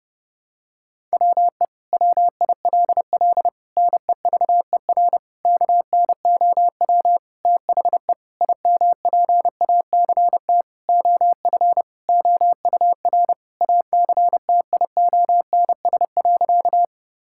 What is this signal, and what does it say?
Morse code, heard as a keyed tone, about 30 wpm, 700 Hz. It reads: WE WILL NEVER KNOW THE IMPACT OF OUR ACTIONS.
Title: Sentences from Top 1000 Words - Review Part 6 - 30wpm